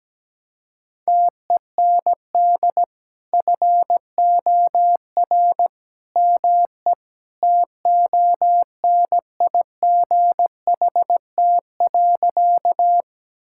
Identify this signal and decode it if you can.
Morse code, heard as a keyed tone, about 17 wpm, 700 Hz. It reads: TEND FOR ME TONIGHT.
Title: Sentences from Top 1000 Words - Review Part 5 - 17wpm